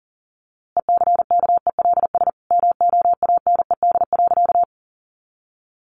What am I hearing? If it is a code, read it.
Morse code, heard as a keyed tone, about 40 wpm, 700 Hz. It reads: ECKELS MOANED.
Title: A Sound Of Thunder 40wpm